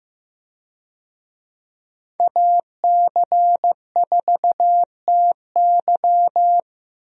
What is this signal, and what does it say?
Morse code, heard as a keyed tone, about 15 wpm, 700 Hz. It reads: AC4TY
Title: USA Call Signs - 5 Characters - 15wpm